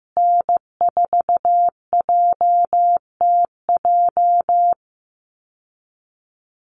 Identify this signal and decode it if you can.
Morse code, heard as a keyed tone, about 15 wpm, 700 Hz. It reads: N4JTJ
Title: USA Call Signs Repeated 3x 15wpm